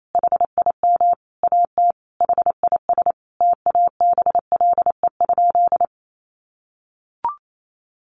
Morse code, heard as a keyed tone, about 28 wpm, 700 Hz. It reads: HIM AT HIS TABLE?
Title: Sentences from Top 1500 Words 28wpm